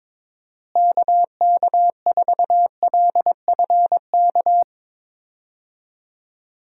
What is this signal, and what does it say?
Morse code, heard as a keyed tone, about 22 wpm, 700 Hz. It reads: KK4LFK
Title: USA Call Signs - 6 Characters - Repeated 3x - 22wpm